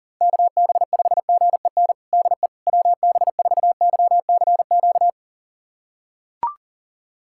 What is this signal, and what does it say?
Morse code, heard as a keyed tone, about 40 wpm, 700 Hz. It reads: KB5GEN DE WB4YCQ
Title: USA Callsign DE Callsign - 6 Characters - 40wpm